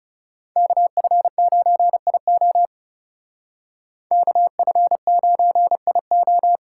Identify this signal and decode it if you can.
Morse code, heard as a keyed tone, about 38 wpm, 700 Hz. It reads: KF9IO KF9IO